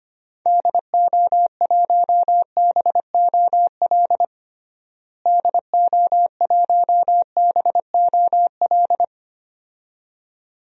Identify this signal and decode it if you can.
Morse code, heard as a keyed tone, about 25 wpm, 700 Hz. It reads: DO1BOL DO1BOL
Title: DXCC-100 Call Signs - 6 Characters - Repeated 2x - 25wpm